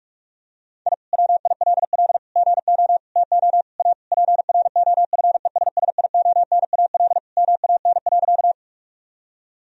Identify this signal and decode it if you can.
Morse code, heard as a keyed tone, about 46 wpm, 700 Hz. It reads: I WILL GO TO A PROFESSIONAL MAN.
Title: Sentences from Top 1000 Words - Review Part 4 - 45wpm